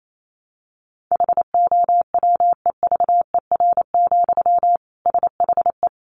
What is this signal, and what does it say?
Morse code, heard as a keyed tone, about 28 wpm, 700 Hz. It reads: HOWEVER, SHE
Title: Sentences from Top 700 Words 28wpm